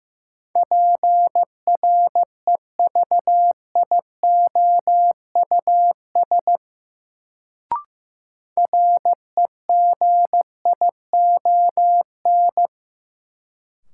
Morse code, heard as a keyed tone, about 15 wpm, 700 Hz. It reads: PREVIOUS REGION
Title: Sentences from Top 1500 Words - New Word Drill - 15wpm